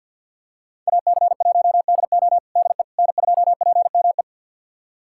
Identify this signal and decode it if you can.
Morse code, heard as a keyed tone, about 48 wpm, 700 Hz. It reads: AC1DO DE N2PME
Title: USA Callsign DE Callsign - 5 Characters - 50wpm